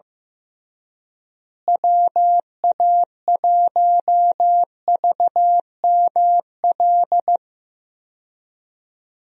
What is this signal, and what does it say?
Morse code, heard as a keyed tone, about 15 wpm, 700 Hz. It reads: WA1VML
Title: USA Call Signs - 6 Characters - Repeated 3x - 15wpm